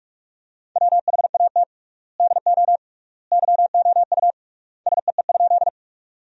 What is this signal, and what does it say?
Morse code, heard as a keyed tone, about 46 wpm, 700 Hz. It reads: WHAT DO YOU SEE?
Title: 3x Word-Spacing - Sentences from Top 100 Words - 45wpm